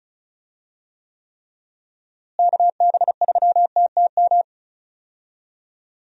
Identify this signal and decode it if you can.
Morse code, heard as a keyed tone, about 34 wpm, 700 Hz. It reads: KB3TTM
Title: USA Call Signs - 6 Characters - Repeated 3x - 35wpm